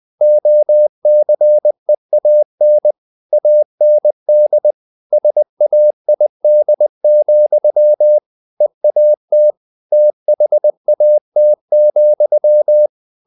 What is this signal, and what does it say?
Morse code, heard as a keyed tone, about 20 wpm, 600 Hz. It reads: OCEAN AND SAID, EAT THAT,
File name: the old man and the sea part 2 20wpm